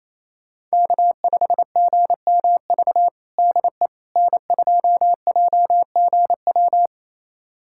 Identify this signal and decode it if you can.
Morse code, heard as a keyed tone, about 28 wpm, 700 Hz. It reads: K5GMV DE N2JGW
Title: USA Callsign DE Callsign - 5 Characters - 28wpm